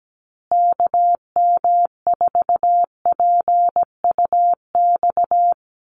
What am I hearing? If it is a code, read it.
Morse code, heard as a keyed tone, about 17 wpm, 700 Hz. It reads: KM4PUX
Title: USA Call Signs Repeated 3x 17wpm